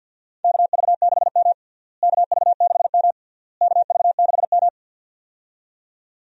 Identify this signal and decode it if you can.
Morse code, heard as a keyed tone, about 50 wpm, 700 Hz. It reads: KV6M KV6M KV6M
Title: USA Call Signs - 4 Characters - Repeated 3x - 50wpm